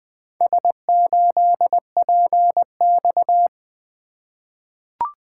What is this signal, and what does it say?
Morse code, heard as a keyed tone, about 20 wpm, 700 Hz. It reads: S8PX
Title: Four Letter-Number Combinations - 20wpm